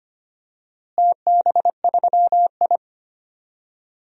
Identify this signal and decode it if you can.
Morse code, heard as a keyed tone, about 25 wpm, 700 Hz. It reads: TB3I
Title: DXCC-100 Call Signs - 4 Characters - Repeated 3x - 25wpm